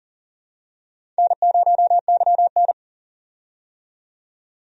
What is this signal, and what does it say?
Morse code, heard as a keyed tone, about 40 wpm, 700 Hz. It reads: N0YN
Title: USA Call Signs - 4 Characters - 40wpm